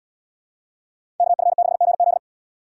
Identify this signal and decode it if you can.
Morse code, heard as a keyed tone, about 72 wpm, 690 Hz. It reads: KF6GZ